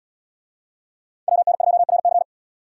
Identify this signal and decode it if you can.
Morse code, heard as a keyed tone, about 70 wpm, 700 Hz. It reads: WN2RC